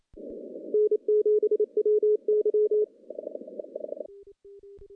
Morse code, heard as a keyed tone, about 28 wpm, 420 Hz. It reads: N7WY
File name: contests / CW-SS-k5tr- -1300